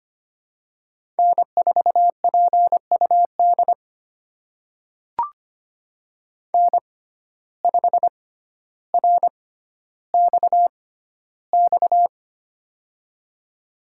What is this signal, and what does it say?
Morse code, heard as a keyed tone, about 25 wpm, spaced as for 13 wpm, 700 Hz. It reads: N4PUD N5RXX
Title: USA Call Signs - 5 Characters - 6x Character Spacing - 25wpm